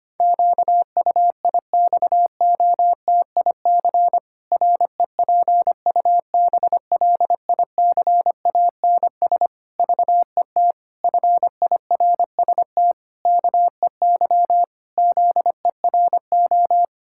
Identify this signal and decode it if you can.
Morse code, heard as a keyed tone, about 25 wpm, 700 Hz. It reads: QUIXOTIC REPUBLICANS VET FIRST KEY ZERO
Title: Pangram Challenge v2 - 25wpm